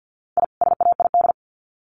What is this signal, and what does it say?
Morse code, heard as a keyed tone, about 50 wpm, 700 Hz. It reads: I SAID
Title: Sets of 2 Words 50wpm